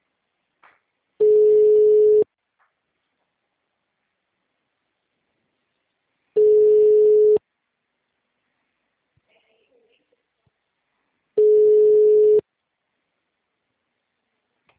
Ringback tone (Europe, 425 Hz).
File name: out- -2050- - - .28411